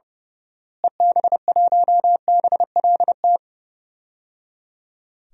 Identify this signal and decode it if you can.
Morse code, heard as a keyed tone, about 30 wpm, 700 Hz. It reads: EB1BLT
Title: DXCC-100 Call Signs - 6 Characters - Repeated 3x - 30wpm